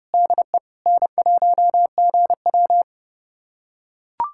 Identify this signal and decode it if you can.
Morse code, heard as a keyed tone, about 30 wpm, 700 Hz. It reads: DE N1GW
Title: USA Callsign DE Callsign - 4 Characters - 30wpm